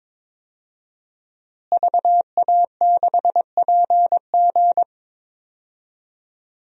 Morse code, heard as a keyed tone, about 22 wpm, 700 Hz. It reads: VA6PG
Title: DXCC-100 Call Signs - 5 Characters - Repeated 2x - 22wpm